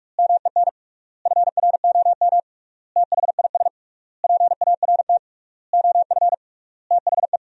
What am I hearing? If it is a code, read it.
Morse code, heard as a keyed tone, about 46 wpm, 700 Hz. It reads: MEN FROM THIS PART OF THE
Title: 3x Word-Spacing - Sentences from Top 300 Words - 45wpm